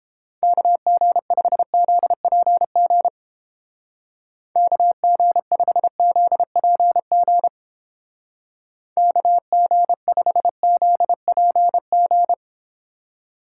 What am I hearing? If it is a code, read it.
Morse code, heard as a keyed tone, about 30 wpm, 700 Hz. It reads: KG5ZPG KG5ZPG KG5ZPG